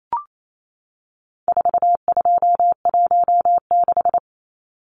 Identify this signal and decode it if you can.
Morse code, heard as a keyed tone, about 28 wpm, 700 Hz. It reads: 4216
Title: Numbers - 4-Digit 28wpm